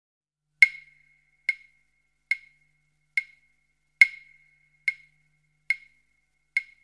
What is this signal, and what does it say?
There is no code.